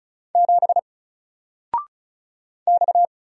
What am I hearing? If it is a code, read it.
Morse code, heard as a keyed tone, about 35 wpm, 700 Hz. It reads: 7 X
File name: Single-Letter-Number-35wpm